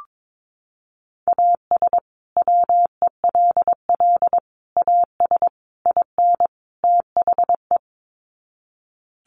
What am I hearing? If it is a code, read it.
Morse code, heard as a keyed tone, about 22 wpm, 700 Hz. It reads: AS WELL AS IN THE
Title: Sets of 5 Words Encore 22wpm